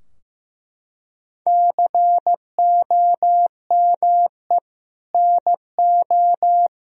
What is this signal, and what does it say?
Morse code, heard as a keyed tone, about 16 wpm, 700 Hz. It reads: COME NO
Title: Sentences from Top 1000 Words - Review Part 4 - 15wpm